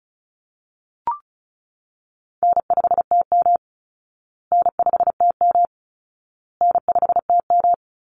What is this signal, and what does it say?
Morse code, heard as a keyed tone, about 34 wpm, 700 Hz. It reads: N5TM N5TM N5TM
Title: Easy Call Signs Repeated 3x - 35wpm